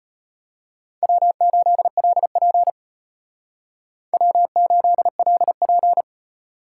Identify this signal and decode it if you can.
Morse code, heard as a keyed tone, about 33 wpm, 700 Hz. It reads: W8LP W8LP